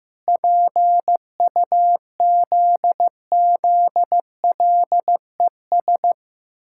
Morse code, heard as a keyed tone, about 15 wpm, 700 Hz. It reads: PUZZLES
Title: Pangram Challenge v2 - 15wpm